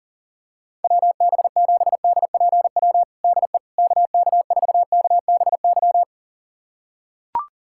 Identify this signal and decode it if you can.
Morse code, heard as a keyed tone, about 40 wpm, 700 Hz. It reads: WB7DPW DE KK4KBY
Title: USA Callsign DE Callsign - 6 Characters - 40wpm